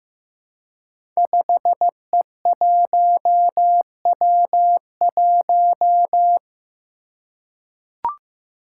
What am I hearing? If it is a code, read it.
Morse code, heard as a keyed tone, about 15 wpm, 700 Hz. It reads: 5E1W1
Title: Five Challenging Letter-Number Combinations 15wpm